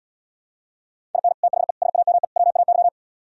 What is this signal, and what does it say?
Morse code, heard as a keyed tone, about 77 wpm, 700 Hz. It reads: IN THE SAME WAY